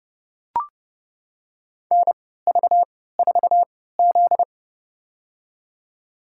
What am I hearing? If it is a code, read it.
Morse code, heard as a keyed tone, about 30 wpm, spaced as for 17 wpm, 700 Hz. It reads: NV4Z
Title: USA Call Signs - 4 Characters - 3x Character Spacing - 30wpm